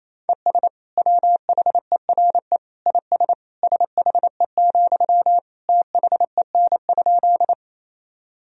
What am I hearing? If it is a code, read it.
Morse code, heard as a keyed tone, about 27 wpm, 700 Hz. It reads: ES WHERE IS SHE, THEN?
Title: Sentences from Top 200 Words and Abbreviations 28wpm